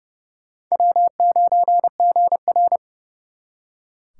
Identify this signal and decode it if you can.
Morse code, heard as a keyed tone, about 36 wpm, 700 Hz. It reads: W9GR
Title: Speed Racing - USA 1x2 Call Signs 30wpm